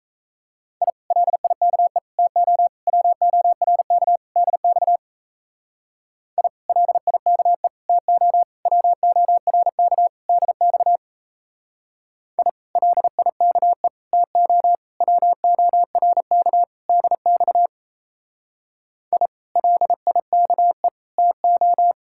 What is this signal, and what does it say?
Morse code, heard as a keyed tone, about 34 wpm, 700 Hz. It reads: I LIKE TO WORK DX I LIKE TO WORK DX I LIKE TO WORK DX I LIKE TO
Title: Speed-Racing Common QSO Phrases Encore - 28wpm